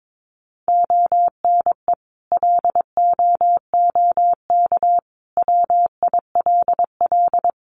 Morse code, heard as a keyed tone, about 22 wpm, 700 Hz. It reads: ONE LOOK WILL
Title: Sentences from Top 1400 Words 22wpm